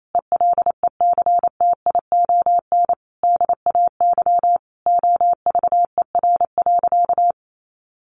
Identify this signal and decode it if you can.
Morse code, heard as a keyed tone, about 28 wpm, 700 Hz. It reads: ELECTION DAY OVER.
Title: A Sound Of Thunder 28wpm